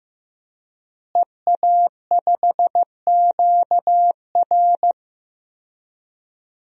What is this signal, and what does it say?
Morse code, heard as a keyed tone, about 15 wpm, 700 Hz. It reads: EA5QR
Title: DXCC-100 Call Signs - 5 Characters - Repeated 2x - 15wpm